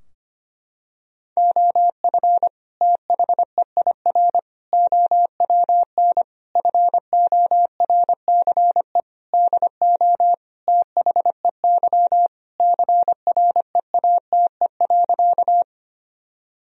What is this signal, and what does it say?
Morse code, heard as a keyed tone, about 25 wpm, 700 Hz. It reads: OF THEIR OWN FORCE DO THEY CREATE.
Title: Sentences from Top 1000 Words - Review Part 3 - 25wpm